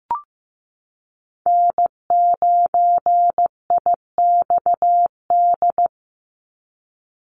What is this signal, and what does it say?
Morse code, heard as a keyed tone, about 15 wpm, 700 Hz. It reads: N9IXD
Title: Call Signs Repeated 2x - 15wpm